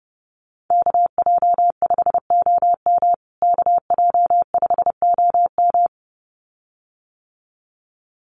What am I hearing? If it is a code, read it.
Morse code, heard as a keyed tone, about 30 wpm, 700 Hz. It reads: KJ5OM KJ5OM
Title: USA Call Signs Repeated 2x 30wpm